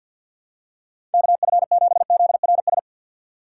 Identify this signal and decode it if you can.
Morse code, heard as a keyed tone, about 53 wpm, 700 Hz. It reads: KF7ZRS
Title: USA Call Signs - 6 Characters - 6x Character Spacing - 50wpm